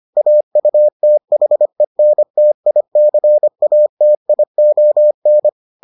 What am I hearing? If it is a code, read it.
Morse code, heard as a keyed tone, about 25 wpm, 600 Hz. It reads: AUTHENTICATION